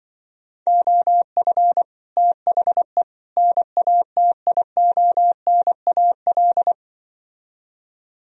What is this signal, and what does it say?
Morse code, heard as a keyed tone, about 25 wpm, 700 Hz. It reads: OF THE NATIONAL
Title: Speed-Racing - Sets of 3 Words 20wpm